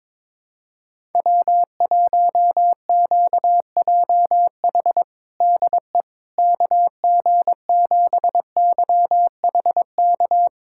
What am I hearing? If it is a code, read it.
Morse code, heard as a keyed tone, about 22 wpm, 700 Hz. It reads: W1QJH DE KG7YHK
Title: USA Callsign DE Callsign 22wpm